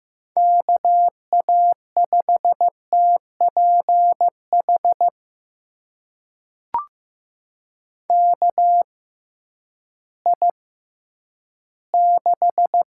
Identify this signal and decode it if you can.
Morse code, heard as a keyed tone, about 15 wpm, spaced as for 10 wpm, 700 Hz. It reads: KA5TPH KI6
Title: USA Call Signs - 6 Characters - 6x Character Spacing - 15wpm